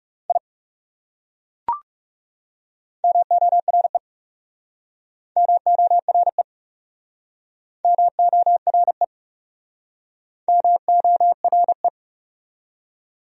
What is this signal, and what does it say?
Morse code, heard as a keyed tone, about 35 wpm, 700 Hz. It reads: I MORE MORE MORE MORE
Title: Speed-Racing Top 100 Words 30wpm